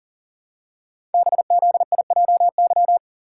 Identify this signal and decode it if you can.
Morse code, heard as a keyed tone, about 39 wpm, 700 Hz. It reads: BZIJY
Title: Code Groups - 5 Letters - 40wpm